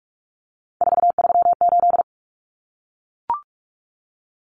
Morse code, heard as a keyed tone, about 44 wpm, 700 Hz. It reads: 438
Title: Common Cut Numbers - 3-Digit 45wpm